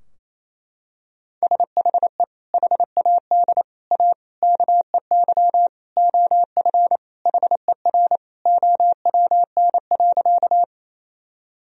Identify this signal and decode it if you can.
Morse code, heard as a keyed tone, about 27 wpm, 700 Hz. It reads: SHE HAD A KEY OF HER OWN.
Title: Sentences from Top 1000 Words - Review Part 4 - 28wpm